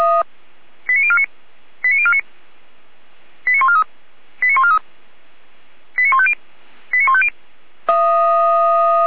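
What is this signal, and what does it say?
Five-tone selective call (ZVEI), heard as quick runs of five tones, tones of about 70 ms. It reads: 89049 89049 89143 89143 89150 89150
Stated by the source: ZVEI five-tone call received by SDR.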